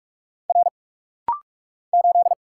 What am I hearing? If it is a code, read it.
Morse code, heard as a keyed tone, about 44 wpm, 700 Hz. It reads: R 8
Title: Single Letter-Number Alt - Rapid-Fire 45wpm